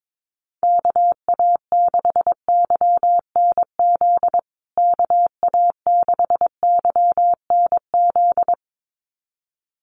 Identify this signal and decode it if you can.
Morse code, heard as a keyed tone, about 22 wpm, 700 Hz. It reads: KA6YNZ KA6YNZ
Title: USA Call Signs Repeated 2x 22wpm